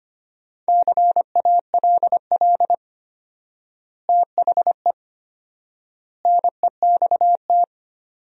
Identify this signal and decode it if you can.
Morse code, heard as a keyed tone, about 25 wpm, 700 Hz. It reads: CALL THE NEXT
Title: 4x Word-Spacing - Sentences from Top 300 Words - 25wpm